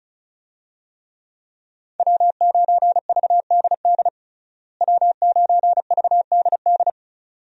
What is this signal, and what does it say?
Morse code, heard as a keyed tone, about 36 wpm, 700 Hz. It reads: W9VDD W9VDD
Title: USA Call Signs - 5 Characters - Repeated 2x - 35wpm